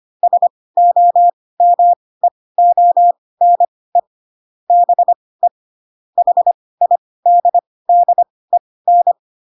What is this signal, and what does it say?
Morse code, heard as a keyed tone, about 25 wpm, spaced as for 18 wpm, 700 Hz. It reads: SOMEONE BE HIDDEN
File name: Pinocchio / Pinocchio ch1 12 25 ik1hge